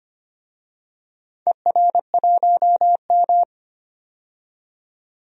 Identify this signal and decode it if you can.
Morse code, heard as a keyed tone, about 25 wpm, 700 Hz. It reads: ER1M
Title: DXCC-100 Call Signs - 4 Characters - 25wpm